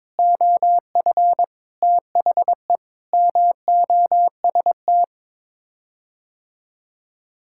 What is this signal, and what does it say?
Morse code, heard as a keyed tone, about 22 wpm, 700 Hz. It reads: OF THE MOST
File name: practice / 22wpm / Sets of 8 Words - 22wpm